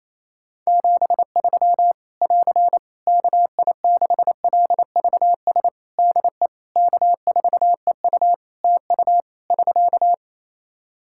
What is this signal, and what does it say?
Morse code, heard as a keyed tone, about 27 wpm, 700 Hz. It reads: 73 <AR> KI6LVS DE K4EU TU <SK>
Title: K1USN SST QSO Practice 28wpm